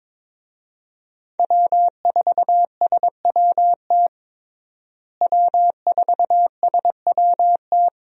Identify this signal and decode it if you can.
Morse code, heard as a keyed tone, about 22 wpm, 700 Hz. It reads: W4SWT W4SWT
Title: USA Call Signs - 5 Characters - Repeated 2x - 22wpm